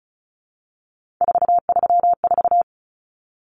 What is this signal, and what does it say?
Morse code, heard as a keyed tone, about 34 wpm, 700 Hz. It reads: 434
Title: Common Cut Numbers - 3-Digit 35wpm